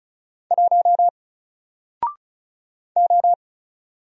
Morse code, heard as a keyed tone, about 36 wpm, 700 Hz. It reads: 1 O
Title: Single Letter-Number Alt 35wpm